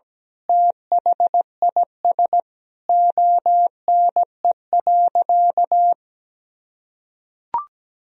Morse code, heard as a keyed tone, about 17 wpm, 700 Hz. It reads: THIS ONE.
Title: Sentences from Top 1000 Words - Review Part 4 - 17wpm